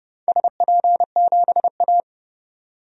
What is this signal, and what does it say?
Morse code, heard as a keyed tone, about 29 wpm, 700 Hz. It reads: SP7A
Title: DXCC-100 Call Signs - 4 Characters - 30wpm